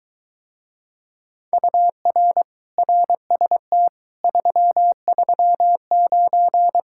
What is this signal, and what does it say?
Morse code, heard as a keyed tone, about 23 wpm, 700 Hz. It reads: UR RST 339